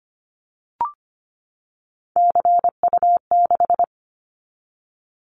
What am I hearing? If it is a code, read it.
Morse code, heard as a keyed tone, about 25 wpm, 700 Hz. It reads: CU6
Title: Callsign Prefixes for Top 100 DXCC - 25wpm